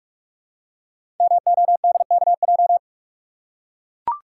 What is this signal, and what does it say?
Morse code, heard as a keyed tone, about 44 wpm, 700 Hz. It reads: MODKJ
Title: Code Groups - 5 Letter:Numbers - 45wpm